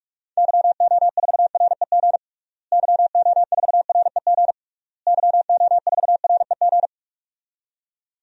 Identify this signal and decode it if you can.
Morse code, heard as a keyed tone, about 41 wpm, 700 Hz. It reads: YO4REG YO4REG YO4REG
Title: DXCC-100 Call Signs - 6 Characters - Repeated 3x - 45wpm